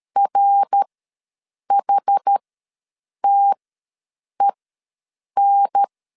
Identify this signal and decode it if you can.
Morse code, heard as a keyed tone, about 13 wpm, spaced as for 7 wpm, 800 Hz. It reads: RHTEN
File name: LEZIONI / 5 LBVC / 5 L ws 12wpm 2958